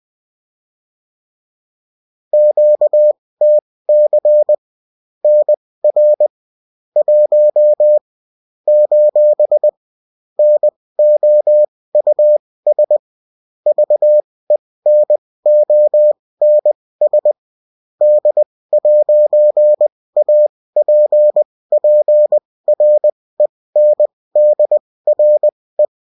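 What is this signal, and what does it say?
Morse code, heard as a keyed tone, about 20 wpm, spaced as for 16 wpm, 600 Hz. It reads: QTC NR 1 : NOUS VENONS D'APPRENDRE
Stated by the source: Bulletin in text and CW form.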